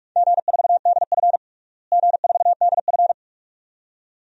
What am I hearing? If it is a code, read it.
Morse code, heard as a keyed tone, about 42 wpm, 700 Hz. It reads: G4DF G4DF
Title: DXCC-100 Call Signs - 4 Characters - Repeated 2x - 45wpm